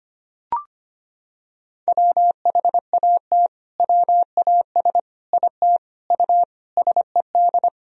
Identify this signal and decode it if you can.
Morse code, heard as a keyed tone, about 25 wpm, 700 Hz. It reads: WHAT WAS IT U SED
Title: Sentences from Top 200 Words and Abbreviations 25wpm